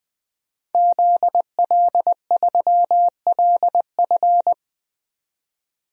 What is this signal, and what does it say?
Morse code, heard as a keyed tone, about 20 wpm, 700 Hz. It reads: ZL3LF
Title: DXCC-100 Call Signs - 5 Characters - 20wpm